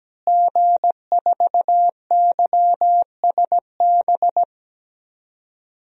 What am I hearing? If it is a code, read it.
Morse code, heard as a keyed tone, about 17 wpm, 700 Hz. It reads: G4YSB
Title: DXCC-100 Call Signs - 5 Characters - Repeated 2x - 17wpm